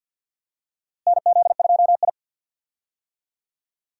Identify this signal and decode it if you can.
Morse code, heard as a keyed tone, about 47 wpm, 700 Hz. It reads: NZ2I
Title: USA Call Signs - 4 Characters - Repeated 2x - 50wpm